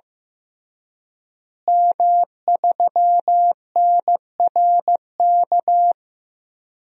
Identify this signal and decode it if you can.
Morse code, heard as a keyed tone, about 16 wpm, 700 Hz. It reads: M3NRK
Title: DXCC-100 Call Signs - 5 Characters - Repeated 3x - 15wpm